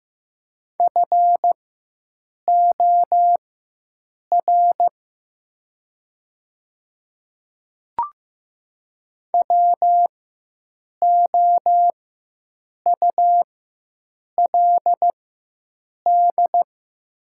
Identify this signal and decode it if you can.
Morse code, heard as a keyed tone, about 15 wpm, spaced as for 7 wpm, 700 Hz. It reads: FOR WOULD
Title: 4x Character Spacing - Top 100 Words - 15wpm